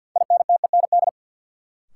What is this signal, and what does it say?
Morse code, heard as a keyed tone, about 53 wpm, 700 Hz. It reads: INTEND
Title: Sentences from Top 1600 Words - New Word Drill - 50wpm